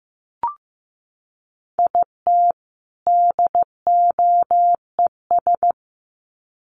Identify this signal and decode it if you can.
Morse code, heard as a keyed tone, about 15 wpm, 700 Hz. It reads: IT DOES
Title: Sets of 2 Words 15wpm